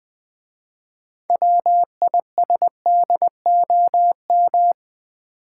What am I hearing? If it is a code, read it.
Morse code, heard as a keyed tone, about 20 wpm, 700 Hz. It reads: WISDOM
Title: 6 Letter Words - First 3 Letters Given - 20wpm